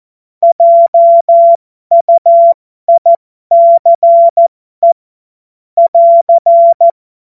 Morse code, heard as a keyed tone, about 14 wpm, spaced as for 12 wpm, 680 Hz. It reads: JUICE <AR>
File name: Morsekurs / Witze